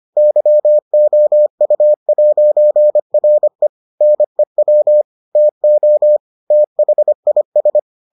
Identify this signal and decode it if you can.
Morse code, heard as a keyed tone, about 25 wpm, 600 Hz. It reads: YOU'RE NEW TO THIS